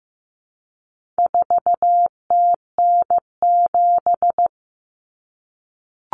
Morse code, heard as a keyed tone, about 15 wpm, 700 Hz. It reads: 4TN7
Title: Common Cut Numbers - 4-Digit 15wpm